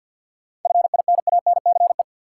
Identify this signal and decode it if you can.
Morse code, heard as a keyed tone, about 53 wpm, 700 Hz. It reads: FINANCE
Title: Sentences from Top 1500 Words - New Word Drill Alt - 50wpm